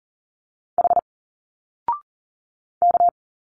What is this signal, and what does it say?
Morse code, heard as a keyed tone, about 39 wpm, 700 Hz. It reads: H K
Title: Single Letters 40wpm